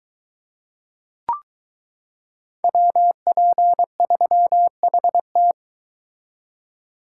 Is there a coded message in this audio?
Yes: Morse code from 2.6 to 5.5 s, heard as a keyed tone, about 22 wpm, 700 Hz. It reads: WP3HT